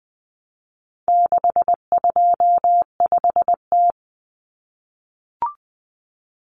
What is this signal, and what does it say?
Morse code, heard as a keyed tone, about 20 wpm, 700 Hz. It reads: 625T
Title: Common Cut Numbers - 4-Digit 20wpm